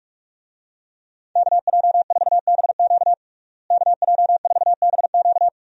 Morse code, heard as a keyed tone, about 48 wpm, 700 Hz. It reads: KJ4BQ KJ4BQ
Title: USA Call Signs - 5 Characters - Repeated 2x - 45wpm